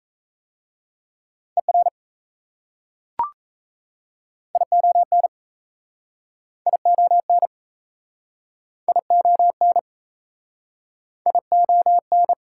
Morse code, heard as a keyed tone, about 38 wpm, 690 Hz. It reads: ER ION ION ION ION